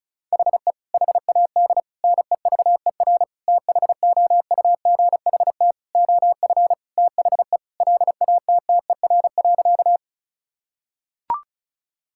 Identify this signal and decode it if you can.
Morse code, heard as a keyed tone, about 36 wpm, 700 Hz. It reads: HE HAD NEVER THOUGHT OF THE LATTER.
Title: Sentences from Top 1600 Words 35wpm